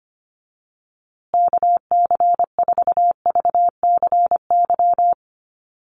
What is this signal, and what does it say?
Morse code, heard as a keyed tone, about 25 wpm, 700 Hz. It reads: KC4VCY
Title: USA Call Signs Repeated 2x 25wpm